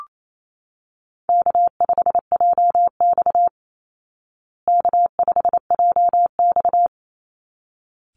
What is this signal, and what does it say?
Morse code, heard as a keyed tone, about 28 wpm, 700 Hz. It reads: K5JX K5JX
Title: Easy Call Signs Repeated 2x - 28wpm